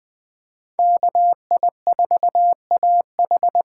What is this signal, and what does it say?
Morse code, heard as a keyed tone, about 20 wpm, 700 Hz. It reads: KI4AH
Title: USA Call Signs - 6 Characters - 8x Character Spacing - 20wpm